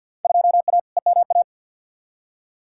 Morse code, heard as a keyed tone, about 47 wpm, 700 Hz. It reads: 2A EMA
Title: Field Day Exchanges Encore - 50wpm